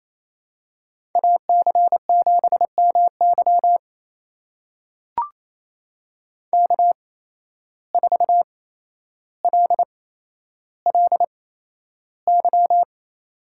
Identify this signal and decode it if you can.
Morse code, heard as a keyed tone, about 28 wpm, spaced as for 12 wpm, 700 Hz. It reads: AC7MY K4LLY
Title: USA Call Signs - 5 Characters - 8x Character Spacing - 28wpm